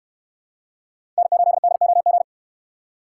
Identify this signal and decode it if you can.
Morse code, heard as a keyed tone, about 66 wpm, 700 Hz. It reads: N8DOG